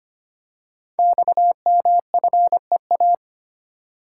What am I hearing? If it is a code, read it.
Morse code, heard as a keyed tone, about 25 wpm, 700 Hz. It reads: XMFEA
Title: Code Groups - 5 Letter:Numbers - 25wpm